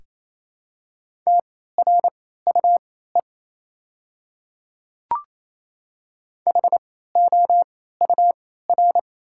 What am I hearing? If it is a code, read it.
Morse code, heard as a keyed tone, about 28 wpm, spaced as for 16 wpm, 690 Hz. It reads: TRUE HOUR
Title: 3x Character Spacing - Top 500 Words - 28wpm